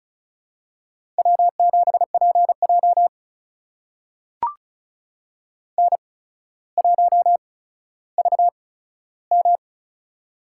Morse code, heard as a keyed tone, about 36 wpm, spaced as for 15 wpm, 700 Hz. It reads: W7PJ N1VM
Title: USA Call Signs - 4 Characters - 8x Character Spacing - 35wpm